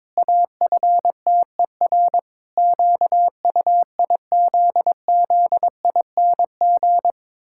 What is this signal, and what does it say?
Morse code, heard as a keyed tone, about 22 wpm, 700 Hz. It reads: AFTER QUIZZING
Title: Pangram Challenge v2 - 22wpm